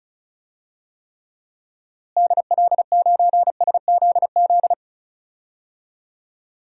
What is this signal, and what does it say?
Morse code, heard as a keyed tone, about 34 wpm, 700 Hz. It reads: DL9SZZ
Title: DXCC-100 Call Signs - 6 Characters - Repeated 2x - 35wpm